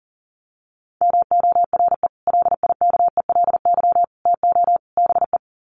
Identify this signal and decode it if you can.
Morse code, heard as a keyed tone, about 37 wpm, 700 Hz. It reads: MORE LIKELY TO BE
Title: Sets of 4 Words 40wpm